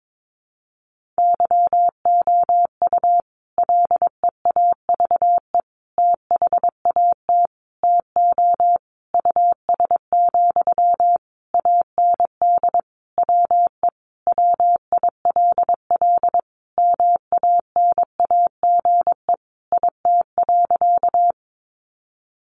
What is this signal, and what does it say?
Morse code, heard as a keyed tone, about 22 wpm, 700 Hz. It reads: YOU LEAVE THAT TO US, AND WE WILL MANAGE IT.
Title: Sentences from Top 1400 Words 22wpm